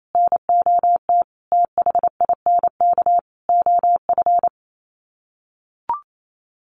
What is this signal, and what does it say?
Morse code, heard as a keyed tone, about 28 wpm, 700 Hz. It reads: NOT THINK OF
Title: Sets of 5 Words 28wpm